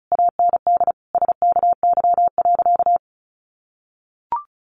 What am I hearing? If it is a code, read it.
Morse code, heard as a keyed tone, about 34 wpm, 700 Hz. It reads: AND SKY.
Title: Sentences from Top 1200 Words 35wpm